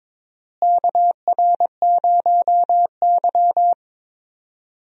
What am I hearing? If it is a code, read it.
Morse code, heard as a keyed tone, about 21 wpm, 700 Hz. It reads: KR0Y
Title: USA Call Signs - 4 Characters - Repeated 2x - 22wpm